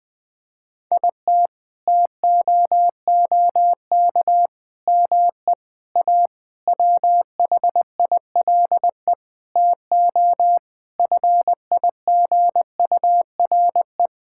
Morse code, heard as a keyed tone, about 20 wpm, 700 Hz. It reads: IT TOOK ME A WHILE TO FIGURE
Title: Sets of 8 Words - 20wpm